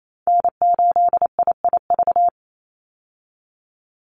Morse code, heard as a keyed tone, about 28 wpm, 700 Hz. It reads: N8IIV
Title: USA Call Signs 28wpm